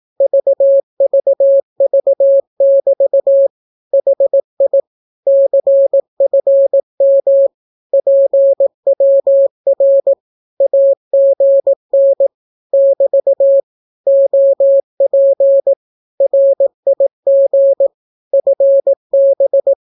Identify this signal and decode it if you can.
Morse code, heard as a keyed tone, about 18 wpm, 550 Hz. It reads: VVV<BT> HI CFM PWR AGN <BT> OP RIG FB
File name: naukapodstaw / skroty / skroty